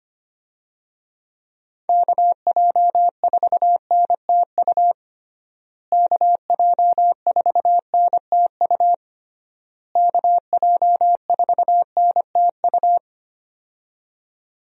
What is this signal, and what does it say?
Morse code, heard as a keyed tone, about 25 wpm, 700 Hz. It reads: KJ4NTU KJ4NTU KJ4NTU
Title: USA Call Signs - 6 Characters - Repeated 3x - 25wpm